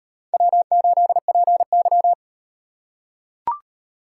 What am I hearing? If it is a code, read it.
Morse code, heard as a keyed tone, about 38 wpm, 700 Hz. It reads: W8PY